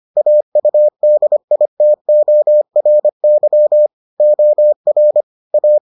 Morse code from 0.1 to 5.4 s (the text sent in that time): AUDITORY OR